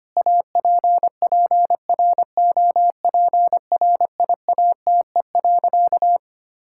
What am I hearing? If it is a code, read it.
Morse code, heard as a keyed tone, about 25 wpm, 700 Hz. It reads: APPROPRIATE.
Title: Sentences from Top 1000 Words - Review Part 5 - 25wpm